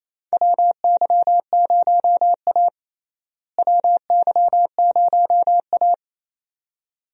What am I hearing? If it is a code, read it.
Morse code, heard as a keyed tone, about 28 wpm, 700 Hz. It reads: WY0A WY0A
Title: USA Call Signs - 4 Characters - Repeated 2x - 28wpm